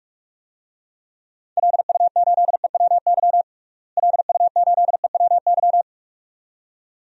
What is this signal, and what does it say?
Morse code, heard as a keyed tone, about 46 wpm, 700 Hz. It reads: LU8EWY LU8EWY
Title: DXCC-100 Call Signs - 6 Characters - Repeated 2x - 45wpm